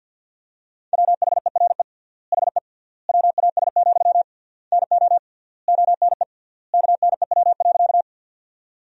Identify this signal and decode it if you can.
Morse code, heard as a keyed tone, about 50 wpm, 700 Hz. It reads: WHERE HE WAS, NO ONE KNEW.
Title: 3x Word-Spacing - Sentences from Top 400 Words - 50wpm